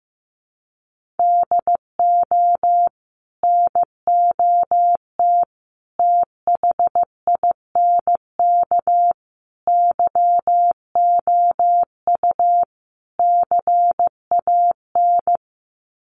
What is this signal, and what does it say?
Morse code, heard as a keyed tone, about 15 wpm, 700 Hz. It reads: DO NOT THINK YOU CAN
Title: Sets of 5 Words 15wpm